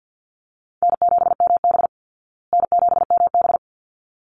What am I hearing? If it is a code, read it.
Morse code, heard as a keyed tone, about 50 wpm, 700 Hz. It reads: N7MB N7MB
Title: Easy Call Signs Repeated 2x - 50wpm